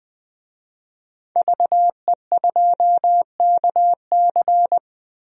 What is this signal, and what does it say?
Morse code, heard as a keyed tone, about 20 wpm, 700 Hz. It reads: VE2KC
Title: DXCC-100 Call Signs - 5 Characters - Repeated 3x - 20wpm